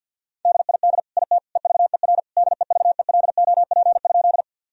Morse code, heard as a keyed tone, about 50 wpm, 700 Hz. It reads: DID IT EVER DEVELOP?
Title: Sentences from Top 1000 Words - Review Part 2 - 50wpm